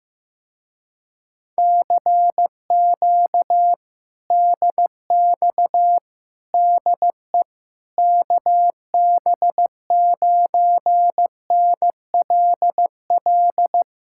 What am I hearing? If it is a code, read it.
Morse code, heard as a keyed tone, about 15 wpm, 700 Hz. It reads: CQ DX DE KB9NLL
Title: Speed-Racing Common QSO Phrases Encore - 15wpm